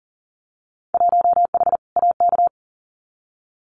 Morse code, heard as a keyed tone, about 40 wpm, 700 Hz. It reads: 1H AK
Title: Winter Field Day Exchanges Encore 40wpm